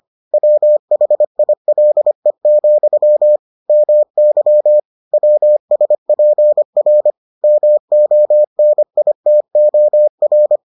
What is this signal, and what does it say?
Morse code, heard as a keyed tone, about 25 wpm, 600 Hz. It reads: WHILE, MY WSPR MONITOR